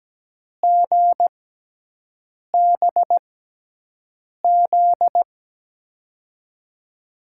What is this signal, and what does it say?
Morse code, heard as a keyed tone, about 17 wpm, spaced as for 6 wpm, 700 Hz. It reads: GBZ
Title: USA Call Signs - 6 Characters - 6x Character Spacing - 17wpm